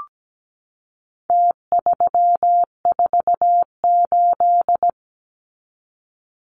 Morse code, heard as a keyed tone, about 17 wpm, 700 Hz. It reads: T348
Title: Numbers - Contest Style - 4-Digit 17wpm